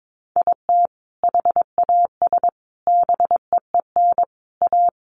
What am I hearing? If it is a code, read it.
Morse code, heard as a keyed tone, about 22 wpm, 700 Hz. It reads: IT HAS BEEN A
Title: Sentences from Top 1100 Words 22wpm